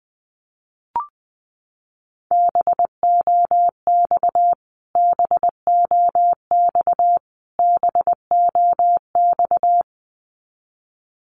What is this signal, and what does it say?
Morse code, heard as a keyed tone, about 20 wpm, 700 Hz. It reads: BOX BOX BOX
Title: Top 500 Words Repeated 20wpm